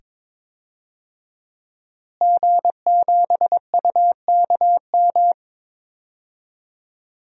Morse code, heard as a keyed tone, about 22 wpm, 700 Hz. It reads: G7UKM
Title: DXCC-100 Call Signs - 5 Characters - Repeated 2x - 22wpm